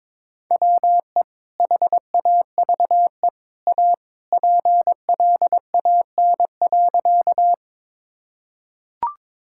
Morse code, heard as a keyed tone, about 22 wpm, 700 Hz. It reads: WE HAVE A PLAN.
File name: practice / 22wpm / Sentences with 2-Word Phrases - 22wpm